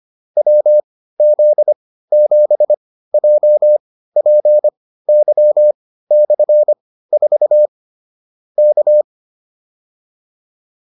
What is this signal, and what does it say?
Morse code, heard as a keyed tone, about 25 wpm, spaced as for 15 wpm, 600 Hz. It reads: WZ7JPY/4 K